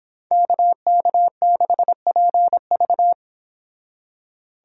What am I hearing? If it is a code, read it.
Morse code, heard as a keyed tone, about 31 wpm, 700 Hz. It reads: KK6PV